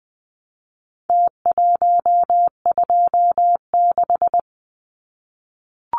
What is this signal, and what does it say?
Morse code, heard as a keyed tone, about 20 wpm, 700 Hz. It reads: T126
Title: Numbers - Contest Style - 4-Digit 20wpm